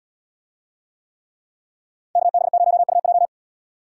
Morse code, heard as a keyed tone, about 72 wpm, 700 Hz. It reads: KB0UJ